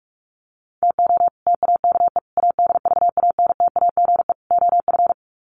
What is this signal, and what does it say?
Morse code, heard as a keyed tone, about 44 wpm, 700 Hz. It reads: TO TAKE ADVANTAGE OF
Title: Sets of 4 Words 45wpm